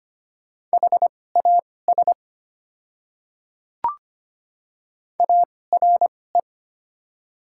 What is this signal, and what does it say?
Morse code, heard as a keyed tone, about 25 wpm, spaced as for 18 wpm, 700 Hz. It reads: HAS ARE
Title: 2x Character Spacing - Top 100 Words - 25wpm